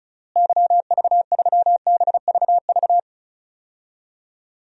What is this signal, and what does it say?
Morse code, heard as a keyed tone, about 36 wpm, 700 Hz. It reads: YV3BVV
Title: DXCC-100 Call Signs - 6 Characters - Repeated 3x - 35wpm